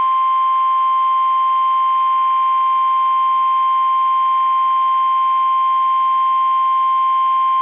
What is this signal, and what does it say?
Emergency Alert System (SAME) tones (the 1050 Hz attention tone). It is heard from 0.7 to 7.0 s.